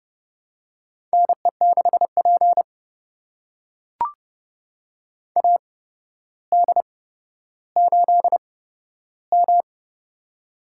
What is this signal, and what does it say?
Morse code, heard as a keyed tone, about 30 wpm, spaced as for 13 wpm, 700 Hz. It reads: NE6P AD8M
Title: USA Call Signs - 4 Characters - 8x Character Spacing - 30wpm